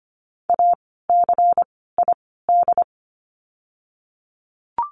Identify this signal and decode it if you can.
Morse code, heard as a keyed tone, about 25 wpm, 700 Hz. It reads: A C I D
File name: course / 25:10wpm / 094 Lesson 16f - Letter c - Words 25wpm